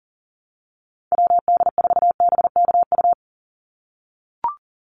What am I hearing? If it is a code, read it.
Morse code, heard as a keyed tone, about 40 wpm, 700 Hz. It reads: WD4BKU
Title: USA Call Signs 40wpm